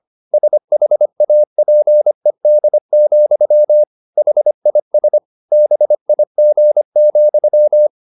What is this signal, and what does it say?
Morse code, heard as a keyed tone, about 25 wpm, 600 Hz. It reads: SHAPED, HIS BIG,
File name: the old man and the sea part 1 25wpm